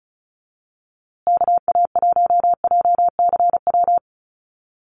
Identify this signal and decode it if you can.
Morse code, heard as a keyed tone, about 34 wpm, 700 Hz. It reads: KA1JCW
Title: USA Call Signs Repeated 2x 35wpm